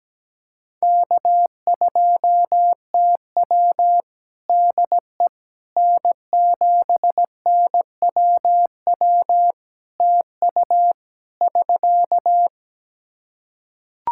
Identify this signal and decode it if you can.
Morse code, heard as a keyed tone, about 17 wpm, 700 Hz. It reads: K2TW DE N7NWW TU <SK>
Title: K1USN SST QSO Practice 17wpm